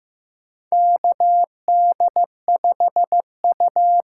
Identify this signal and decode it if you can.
Morse code, heard as a keyed tone, about 15 wpm, 700 Hz. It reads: KD5U
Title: USA Call Signs - 6x Character Spacing - 15wpm